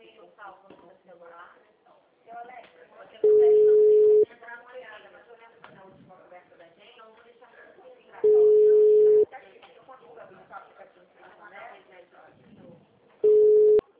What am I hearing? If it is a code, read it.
Ringback tone (Europe, 425 Hz), with speech.
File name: out- -2002- - - .8249